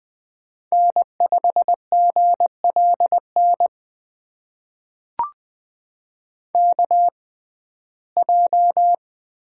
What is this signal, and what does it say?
Morse code, heard as a keyed tone, about 20 wpm, spaced as for 14 wpm, 700 Hz. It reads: N5GLN KJ
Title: USA Call Signs - 5 Characters - 6x Character Spacing - 20wpm